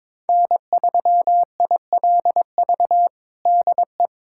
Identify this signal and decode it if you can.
Morse code, heard as a keyed tone, about 23 wpm, 700 Hz. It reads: N3ILV DE
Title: USA Callsign DE Callsign 22wpm